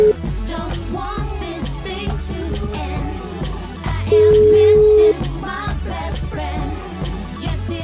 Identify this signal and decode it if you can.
Ringback tone (Europe, 425 Hz).